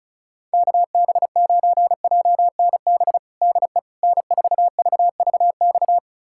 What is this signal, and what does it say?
Morse code, heard as a keyed tone, about 36 wpm, 690 Hz. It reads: KB9JNB DE N4VVX
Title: USA Callsign DE Callsign 35wpm